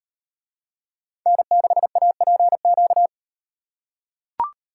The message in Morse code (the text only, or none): N6APQ